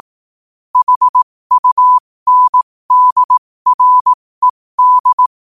Morse code, heard as a keyed tone, about 18 wpm, spaced as for 15 wpm, 1000 Hz. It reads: HUNDRED
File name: qotdfiles / 15wpm